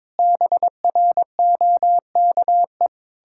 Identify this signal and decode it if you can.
Morse code, heard as a keyed tone, about 22 wpm, 700 Hz. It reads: BROKE
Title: Pangram Challenge v2 - 22wpm